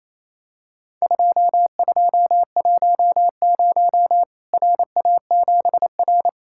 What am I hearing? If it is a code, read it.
Morse code, heard as a keyed tone, about 28 wpm, 700 Hz. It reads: 2210 RA7R
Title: Worked All Europe DX Contest - QTC Exchange - 28wpm